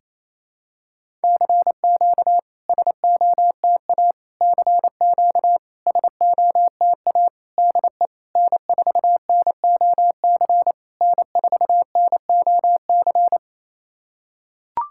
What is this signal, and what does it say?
Morse code, heard as a keyed tone, about 28 wpm, 700 Hz. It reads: CQ SOTA CQ SOTA DE N4NOC N4NOC
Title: SOTA QSO Practice 28wpm